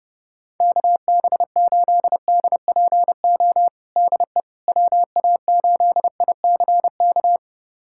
Morse code, heard as a keyed tone, about 30 wpm, 700 Hz. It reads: KB8DPO DE WA8ICK
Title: USA Callsign DE Callsign - 6 Characters - 30wpm